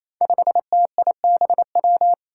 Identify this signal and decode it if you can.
Morse code, heard as a keyed tone, about 28 wpm, 700 Hz. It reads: 5TIBW
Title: Five Challenging Letter-Number Combinations 28wpm